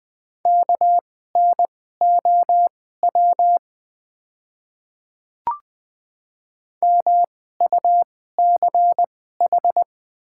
Morse code, heard as a keyed tone, about 20 wpm, spaced as for 14 wpm, 700 Hz. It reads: KNOW MUCH
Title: 2x Character Spacing - Top 200 Words - 20wpm